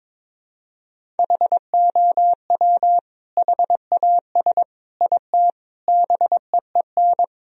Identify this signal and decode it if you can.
Morse code, heard as a keyed tone, about 22 wpm, 700 Hz. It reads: HOW HAS IT BEEN
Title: Sentences from Top 1000 Words - Review Part 1 - 22wpm